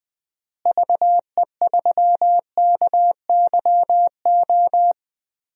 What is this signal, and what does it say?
Morse code, heard as a keyed tone, about 20 wpm, 690 Hz. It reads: VE3KYO
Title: DXCC-100 Call Signs - 6 Characters - Repeated 2x - 20wpm